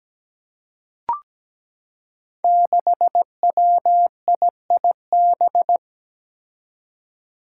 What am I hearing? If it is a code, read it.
Morse code, heard as a keyed tone, about 17 wpm, 700 Hz. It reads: 6WIIB
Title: Five Challenging Letter-Number Combinations 17wpm